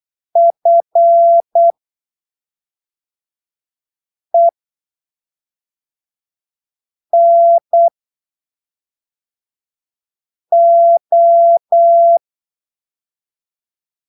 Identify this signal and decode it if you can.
Morse code, heard as a keyed tone, about 8 wpm, spaced as for 3 wpm, 680 Hz. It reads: FENO
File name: texto 24 08ppm